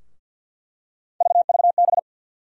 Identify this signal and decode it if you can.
Morse code, heard as a keyed tone, about 47 wpm, 700 Hz. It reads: VVB
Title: USA Call Sign Postfixes - 3 Letters 50wpm